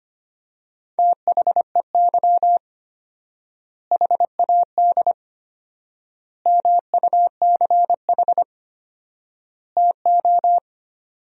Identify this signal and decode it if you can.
Morse code, heard as a keyed tone, about 25 wpm, 700 Hz. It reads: THEY HAD MUCH TO
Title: 4x Word-Spacing - Sentences from Top 400 Words - 25wpm